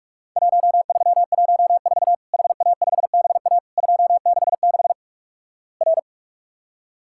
Morse code, heard as a keyed tone, about 46 wpm, 700 Hz. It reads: 1314 HA5BA 266 R
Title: Worked All Europe DX Contest - QTC Exchange - 45wpm